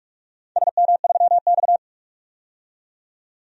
Morse code, heard as a keyed tone, about 46 wpm, 700 Hz. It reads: SM3X
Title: DXCC-100 Call Signs - 4 Characters - 45wpm